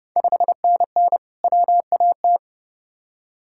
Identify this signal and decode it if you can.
Morse code, heard as a keyed tone, about 30 wpm, 700 Hz. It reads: 5NN WAT
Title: Ontario QSO Party Exchange 30wpm